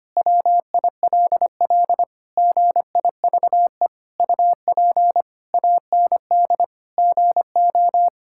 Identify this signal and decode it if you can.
Morse code, heard as a keyed tone, about 25 wpm, 700 Hz. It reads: WILL GIVE UP AND GO
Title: Sentences from Top 1000 Words - Review Part 6 - 25wpm